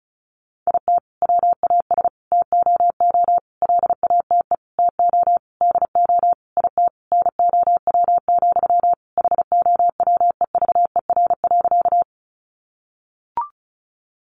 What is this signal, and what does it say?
Morse code, heard as a keyed tone, about 34 wpm, 700 Hz. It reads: IT WAS TOO LATE TO DO IT NOW, HOWEVER.
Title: Sentences from Top 600 Words 35wpm